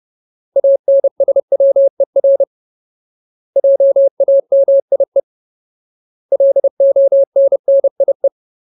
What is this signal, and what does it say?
Morse code, heard as a keyed tone, about 30 wpm, 550 Hz. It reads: ANSWER JAMIE LONNIE
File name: Ultimate Word List / 30WPM Ultimate